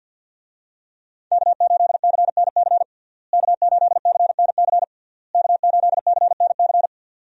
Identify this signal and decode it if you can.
Morse code, heard as a keyed tone, about 52 wpm, 700 Hz. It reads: K8CNC K8CNC K8CNC
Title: USA Call Signs - 5 Characters - Repeated 3x - 50wpm